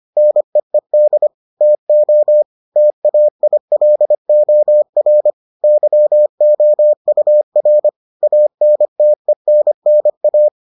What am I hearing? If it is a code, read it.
Morse code, heard as a keyed tone, about 25 wpm, 600 Hz. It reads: NEED TO TAILOR YOUR ANTENNA